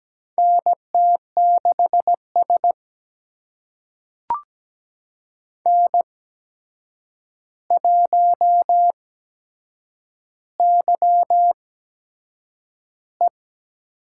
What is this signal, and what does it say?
Morse code, heard as a keyed tone, about 17 wpm, spaced as for 7 wpm, 700 Hz. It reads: NT6S N1YE
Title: USA Call Signs - 4 Characters - 8x Character Spacing - 17wpm